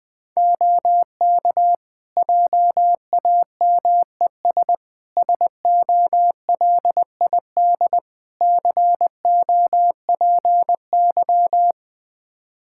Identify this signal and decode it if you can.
Morse code, heard as a keyed tone, about 20 wpm, 700 Hz. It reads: OK JAMES SOLID COPY
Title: Practice QSO - Exchange 3 - More Info - 20wpm